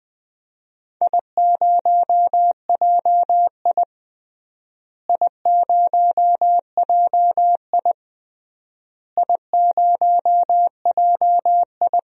Morse code, heard as a keyed tone, about 20 wpm, 700 Hz. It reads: I0JI I0JI I0JI
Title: DXCC-100 Call Signs - 4 Characters - Repeated 3x - 20wpm